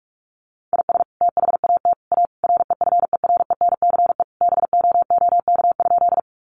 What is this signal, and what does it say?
Morse code, heard as a keyed tone, about 44 wpm, 700 Hz. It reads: IS THAT A REFERENCE BOOK?
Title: Sentences from Top 1200 Words 45wpm